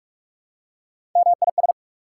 Morse code, heard as a keyed tone, about 47 wpm, 700 Hz. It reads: MIS
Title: Prefixes and Suffixes 45wpm